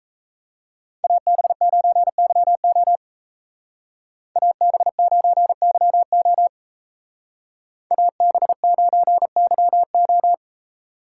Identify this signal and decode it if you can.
Morse code, heard as a keyed tone, about 34 wpm, 700 Hz. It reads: AB9YO AB9YO AB9YO
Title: Speed Racing - USA Call Signs 28wpm